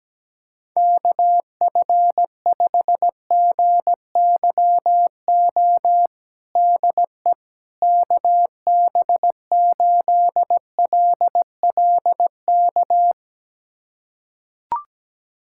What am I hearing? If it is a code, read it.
Morse code, heard as a keyed tone, about 17 wpm, 700 Hz. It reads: KF5GYO DE KB8LLK
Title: USA Callsign DE Callsign 17wpm